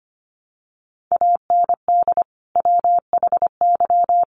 Morse code, heard as a keyed tone, about 25 wpm, 700 Hz. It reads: AND WHY
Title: Sentences from Top 400 Words 25wpm